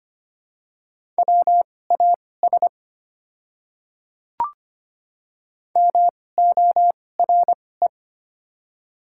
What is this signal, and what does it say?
Morse code, heard as a keyed tone, about 25 wpm, spaced as for 18 wpm, 700 Hz. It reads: WAS MORE
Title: 2x Character Spacing - Top 100 Words - 25wpm